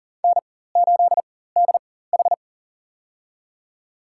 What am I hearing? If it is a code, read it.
Morse code, heard as a keyed tone, about 40 wpm, spaced as for 19 wpm, 700 Hz. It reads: N8DH
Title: USA Call Signs - 4 Characters - 4x Character Spacing - 40wpm